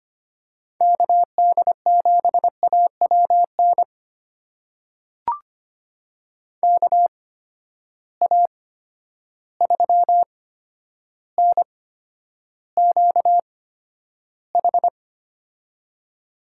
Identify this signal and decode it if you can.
Morse code, heard as a keyed tone, about 25 wpm, spaced as for 11 wpm, 700 Hz. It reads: KD7AWN KA3NQH
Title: USA Call Signs - 6 Characters - 8x Character Spacing - 25wpm